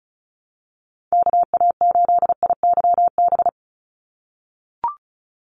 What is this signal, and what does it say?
Morse code, heard as a keyed tone, about 34 wpm, 700 Hz. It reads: KA8IYB
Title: USA Call Signs 35wpm